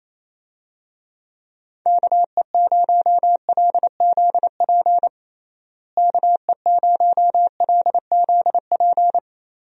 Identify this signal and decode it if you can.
Morse code, heard as a keyed tone, about 28 wpm, 700 Hz. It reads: KE0LZP KE0LZP
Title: USA Call Signs - 6 Characters - Repeated 3x - 28wpm